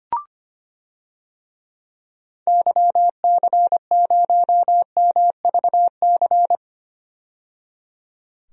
Morse code, heard as a keyed tone, about 25 wpm, 700 Hz. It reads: YC0MVC
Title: DXCC-100 Call Signs - 6 Characters - 25wpm